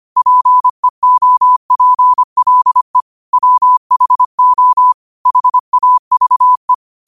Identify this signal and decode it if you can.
Morse code, heard as a keyed tone, about 25 wpm, 1000 Hz. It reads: PEOPLE WHO HAVE
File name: qotdfiles / 25wpm